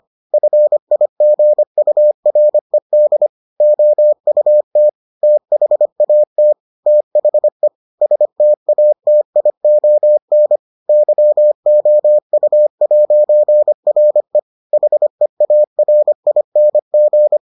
Morse code, heard as a keyed tone, about 25 wpm, 600 Hz. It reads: FIGURED OUT THAT THE STATION YOU'RE HEARING